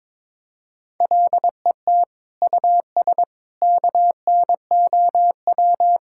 Morse code, heard as a keyed tone, about 22 wpm, 700 Hz. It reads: LET US KNOW
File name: practice / 22wpm / Sets of 6 Words - 22wpm